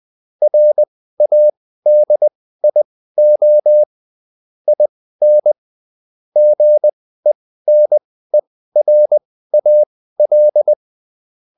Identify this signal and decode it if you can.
Morse code, heard as a keyed tone, about 20 wpm, spaced as for 14 wpm, 600 Hz. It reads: RADIO IN GENERAL